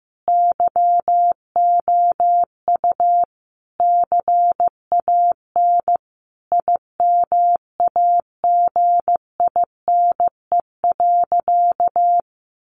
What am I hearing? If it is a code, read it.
Morse code, heard as a keyed tone, about 15 wpm, 700 Hz. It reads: YOU CAN IMAGINE.
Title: Sentences from Top 1500 Words 15wpm